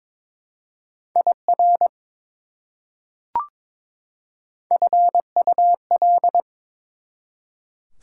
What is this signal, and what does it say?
Morse code, heard as a keyed tone, about 22 wpm, 700 Hz. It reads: IR FUL
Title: Prefixes and Suffixes 22wpm